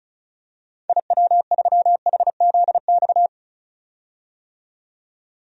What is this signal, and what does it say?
Morse code, heard as a keyed tone, about 36 wpm, 700 Hz. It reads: IW3HZX
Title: DXCC-100 Call Signs - 6 Characters - 35wpm